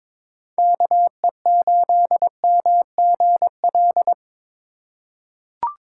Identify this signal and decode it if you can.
Morse code, heard as a keyed tone, about 22 wpm, 700 Hz. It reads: KE8MGL
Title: USA Call Signs - 3x Character Spacing - 22wpm